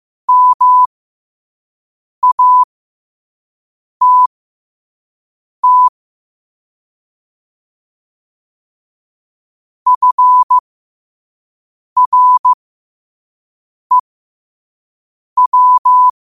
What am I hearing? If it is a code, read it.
Morse code, heard as a keyed tone, about 15 wpm, spaced as for 5 wpm, 1000 Hz. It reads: MATT FREW
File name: qotdfiles / 5wpm